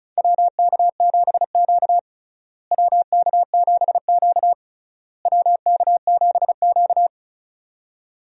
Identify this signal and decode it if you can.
Morse code, heard as a keyed tone, about 36 wpm, 700 Hz. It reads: WK7Q WK7Q WK7Q
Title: USA Call Signs - 4 Characters - Repeated 3x - 35wpm